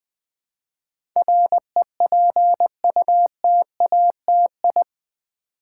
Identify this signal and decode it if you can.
Morse code, heard as a keyed tone, about 20 wpm, 700 Hz. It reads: REPUTATI
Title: Build A Word - 4 Letters At A Time - 10-Letter Words - 20wpm